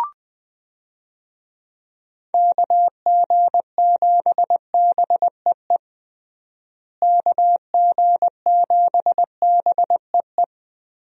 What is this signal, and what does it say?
Morse code, heard as a keyed tone, about 20 wpm, 700 Hz. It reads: KG7BEE KG7BEE
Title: USA Call Signs - 6 Characters - Repeated 3x - 20wpm